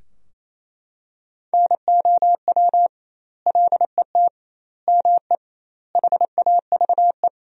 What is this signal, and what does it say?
Morse code, heard as a keyed tone, about 27 wpm, 700 Hz. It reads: NOW LET ME HAVE
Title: 2x Word-Spacing - Sentences from Top 300 Words - 28wpm